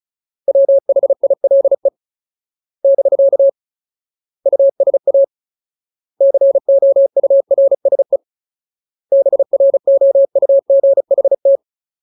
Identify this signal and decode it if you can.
Morse code, heard as a keyed tone, about 35 wpm, 550 Hz. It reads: WHILE <BK> USA COURSE BROUGHT
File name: Ultimate Word List / 35WPM Ultimate